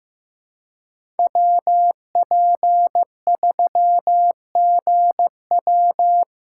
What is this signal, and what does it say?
Morse code, heard as a keyed tone, about 15 wpm, 700 Hz. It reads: WP3GW
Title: USA Call Signs - 5 Characters - 6x Character Spacing - 15wpm